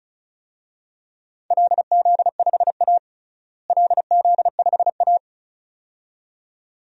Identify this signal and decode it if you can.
Morse code, heard as a keyed tone, about 34 wpm, 700 Hz. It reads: LZ5A LZ5A
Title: DXCC-100 Call Signs - 4 Characters - Repeated 2x - 35wpm